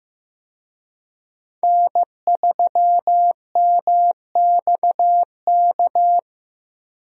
Morse code, heard as a keyed tone, about 15 wpm, 700 Hz. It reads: N3MXK
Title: USA Call Signs - 5 Characters - 15wpm